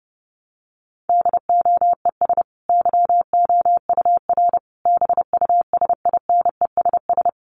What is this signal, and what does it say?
Morse code, heard as a keyed tone, about 30 wpm, 700 Hz. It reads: DOES YOUR BUSINESS
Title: Sentences from Top 1200 Words 30wpm